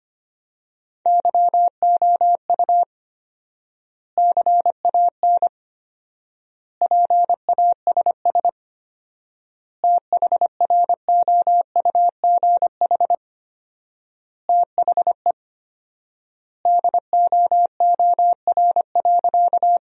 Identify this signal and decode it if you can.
Morse code, heard as a keyed tone, about 25 wpm, 700 Hz. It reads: YOU CAN PASS THROUGH THE DOOR.
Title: 4x Word-Spacing - Sentences from Top 400 Words - 25wpm